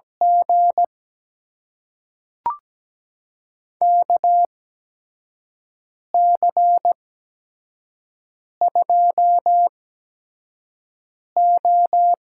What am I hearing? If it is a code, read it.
Morse code, heard as a keyed tone, about 17 wpm, spaced as for 5 wpm, 700 Hz. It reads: G KC2O
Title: USA Call Signs - 6 Characters - 8x Character Spacing - 17wpm